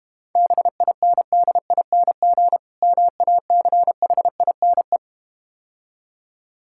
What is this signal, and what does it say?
Morse code, heard as a keyed tone, about 32 wpm, 700 Hz. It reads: BINDING MACHINE